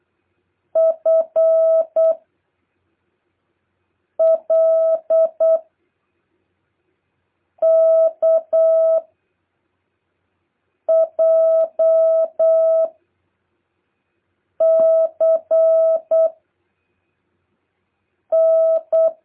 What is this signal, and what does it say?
Morse code, heard as a keyed tone, about 8 wpm, 650 Hz. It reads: F L K J C N
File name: jh8cbh / 2023cw